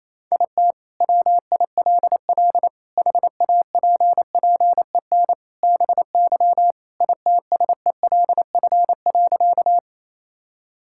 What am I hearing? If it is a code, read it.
Morse code, heard as a keyed tone, about 28 wpm, 700 Hz. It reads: IT WILL HAPPEN BY ITSELF.
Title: Sentences from Top 1000 Words - Review Part 5 - 28wpm